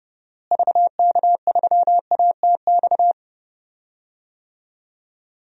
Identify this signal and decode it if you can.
Morse code, heard as a keyed tone, about 30 wpm, 700 Hz. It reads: VK3ATX
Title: DXCC-100 Call Signs - 6 Characters - Repeated 3x - 30wpm